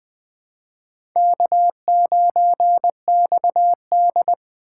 Morse code, heard as a keyed tone, about 20 wpm, 700 Hz. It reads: K9XD
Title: K1USN SST QSO Practice - 20wpm